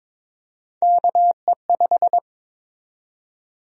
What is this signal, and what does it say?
Morse code, heard as a keyed tone, about 22 wpm, 700 Hz. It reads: KE5
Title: USA Call Sign Prefixes - Repeated 2x - 22wpm